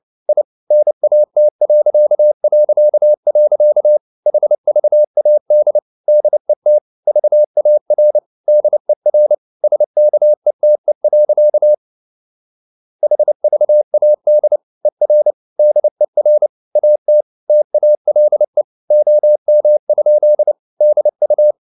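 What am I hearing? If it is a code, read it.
Morse code, heard as a keyed tone, about 29 wpm, 600 Hz. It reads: I NAT... HVAD DET VAR DER SKETE. HVAD ER DER AT TALE OM? DU
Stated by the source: Fast (QRQ) Morse at 29 wpm, from CW med Gnister.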